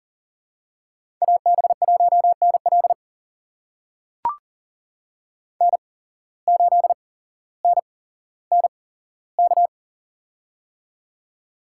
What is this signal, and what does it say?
Morse code, heard as a keyed tone, about 42 wpm, spaced as for 17 wpm, 700 Hz. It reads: AB1NL N8NNK
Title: USA Call Signs - 5 Characters - 8x Character Spacing - 40wpm